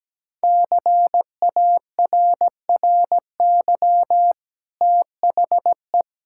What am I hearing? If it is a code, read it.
Morse code, heard as a keyed tone, about 17 wpm, 700 Hz. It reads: CARRY THE
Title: Sentences from Top 1600 Words 17wpm